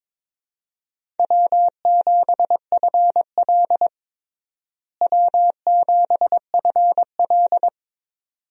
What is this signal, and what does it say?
Morse code, heard as a keyed tone, about 22 wpm, 700 Hz. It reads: W7FL W7FL
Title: USA Call Signs - 4 Characters - Repeated 2x - 22wpm